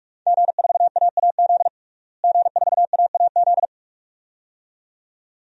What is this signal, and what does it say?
Morse code, heard as a keyed tone, about 42 wpm, 700 Hz. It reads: G4AAZ G4AAZ
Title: DXCC-100 Call Signs - 5 Characters - Repeated 2x - 45wpm